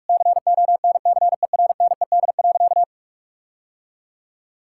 Morse code, heard as a keyed tone, about 48 wpm, 700 Hz. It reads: CONCERNED.
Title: Sentences from Top 1000 Words - Review Part 6 - 45wpm